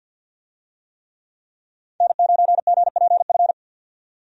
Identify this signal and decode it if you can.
Morse code, heard as a keyed tone, about 52 wpm, 700 Hz. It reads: N9GPF
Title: USA Call Signs - 5 Characters - 50wpm